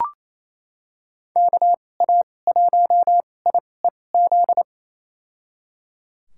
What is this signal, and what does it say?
Morse code, heard as a keyed tone, about 28 wpm, spaced as for 20 wpm, 700 Hz. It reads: KA1IEZ
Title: USA Call Signs - 2x Character Spacing - 28wpm